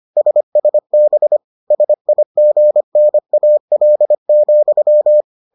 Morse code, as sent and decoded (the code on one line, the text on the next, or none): ... ... -... / ... .. --. -. .- .-.. --..--
SSB SIGNAL,